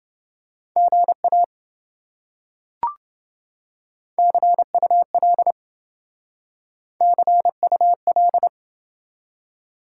Morse code, heard as a keyed tone, about 27 wpm, 700 Hz. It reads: GA CUL CUL